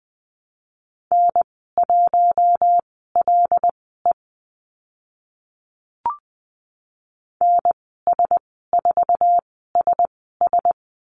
Morse code, heard as a keyed tone, about 20 wpm, spaced as for 14 wpm, 700 Hz. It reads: N1LE NS4SS
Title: 088 Lesson 15g - Number 5 - Call Signs 20wpm